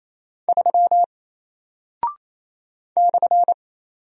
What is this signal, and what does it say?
Morse code, heard as a keyed tone, about 28 wpm, 700 Hz. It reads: 3 /
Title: Single Character (Letter:Number:Punctuation:Prosign) 28wpm